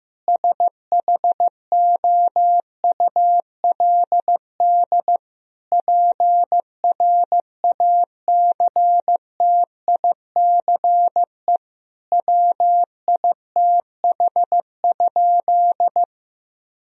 Morse code, heard as a keyed tone, about 15 wpm, 700 Hz. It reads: SHOULD PRACTICE WITH?
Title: Sentences from Top 1000 Words - Review Part 4 - 15wpm